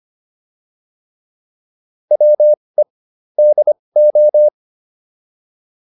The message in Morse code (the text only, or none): WE DO